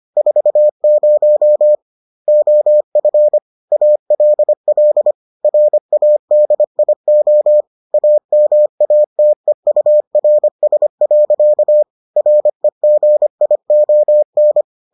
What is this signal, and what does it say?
Morse code, heard as a keyed tone, about 25 wpm, 600 Hz. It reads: 40 OF ALL RADIO AMATEURS. REGION